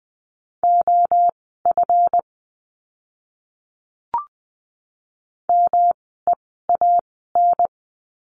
Morse code, heard as a keyed tone, about 20 wpm, spaced as for 15 wpm, 700 Hz. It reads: OF MEAN
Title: 146 Lesson 24f - Letter y - Words 20wpm